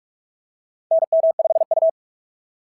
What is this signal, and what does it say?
Morse code, heard as a keyed tone, about 46 wpm, 690 Hz. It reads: NM5U
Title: SOTA QSO Practice 45wpm